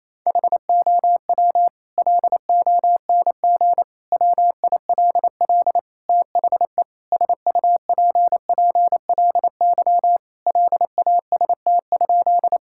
Morse code, heard as a keyed tone, about 28 wpm, 700 Hz. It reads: HOW LONG WILL THE SUPPLY LAST?
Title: Sentences from Top 1000 Words - Review Part 5 - 28wpm